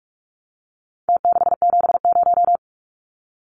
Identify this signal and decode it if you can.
Morse code, heard as a keyed tone, about 44 wpm, 700 Hz. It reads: T670
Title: Numbers - Contest Style - 4-Digit 45wpm